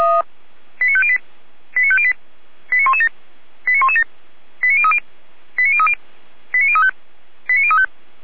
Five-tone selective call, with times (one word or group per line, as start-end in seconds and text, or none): 0.8-1.2 89508
1.8-2.1 89508
2.7-3.1 89117
3.7-4.0 89117
4.6-5.0 89030
5.6-5.9 89030
6.5-6.9 89035
7.5-7.9 89035